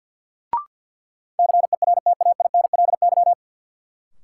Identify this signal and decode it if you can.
Morse code, heard as a keyed tone, about 53 wpm, 700 Hz. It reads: CERTAINLY
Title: Sentences from Top 900 Words - New Word Drill - 50wpm